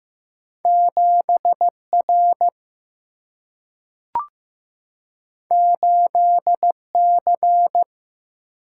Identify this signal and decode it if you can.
Morse code, heard as a keyed tone, about 15 wpm, 700 Hz. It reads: 7R 8C
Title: Single-Letter Single-Number 15wpm